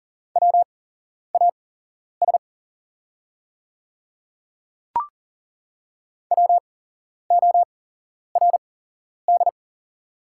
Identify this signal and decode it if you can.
Morse code, heard as a keyed tone, about 38 wpm, spaced as for 11 wpm, 690 Hz. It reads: WAS WORD
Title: 8x Character Spacing - Top 100 Words - 40wpm